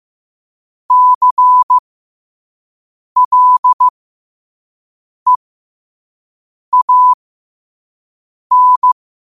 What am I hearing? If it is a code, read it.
Morse code, heard as a keyed tone, about 15 wpm, spaced as for 5 wpm, 1000 Hz. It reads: CLEAN
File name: qotdfiles / 5wpm